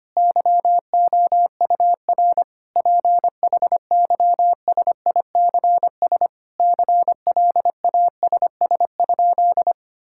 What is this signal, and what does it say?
Morse code, heard as a keyed tone, about 25 wpm, 700 Hz. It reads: YOUR PHYSICS CLASS?
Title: Sentences from Top 1000 Words - Review Part 5 - 25wpm